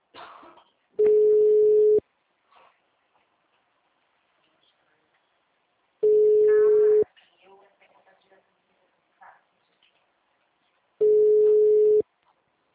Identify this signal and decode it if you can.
Ringback tone (Europe, 425 Hz), with speech.